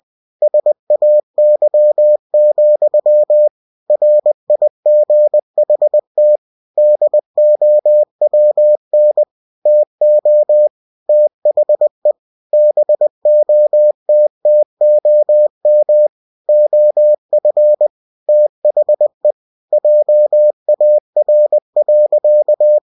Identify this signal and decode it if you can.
Morse code, heard as a keyed tone, about 20 wpm, 600 Hz. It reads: SAY, RIGHT DOWN TO THE BOTTOM OF THE JAR.
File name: cw-books / winnie-the-pooh / winnie-20wpm-page-0037